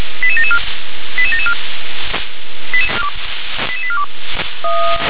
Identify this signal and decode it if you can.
Five-tone selective call (ZVEI), heard as quick runs of five tones, tones of about 70 ms. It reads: 88774 88774 88842
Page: SDR/ZVEI